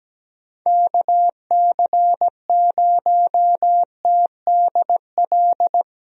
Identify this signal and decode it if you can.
Morse code, heard as a keyed tone, about 17 wpm, 700 Hz. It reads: KC0TDL
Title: USA Call Signs - 6 Characters - 17wpm